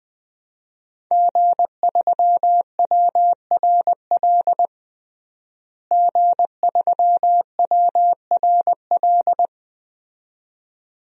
Morse code, heard as a keyed tone, about 20 wpm, 700 Hz. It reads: G3WRL G3WRL
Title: DXCC-100 Call Signs - 5 Characters - Repeated 2x - 20wpm